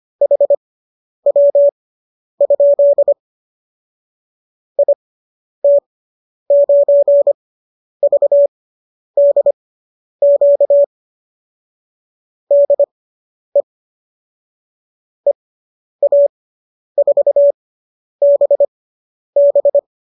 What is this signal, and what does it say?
Morse code, heard as a keyed tone, about 25 wpm, spaced as for 10 wpm, 570 Hz. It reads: HW? IT9VDQ DE EA4BB